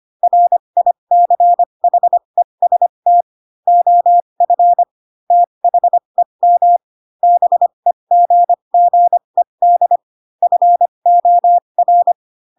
Morse code, heard as a keyed tone, about 25 wpm, spaced as for 22 wpm, 700 Hz. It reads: RICHEST OF THEM BEGGED FOR
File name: Pinocchio / Pinocchio ch3 18 25 ik1hge